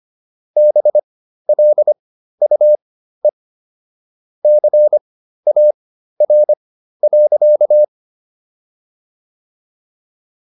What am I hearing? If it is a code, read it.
Morse code, heard as a keyed tone, about 25 wpm, spaced as for 13 wpm, 600 Hz. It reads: BLUE CAR.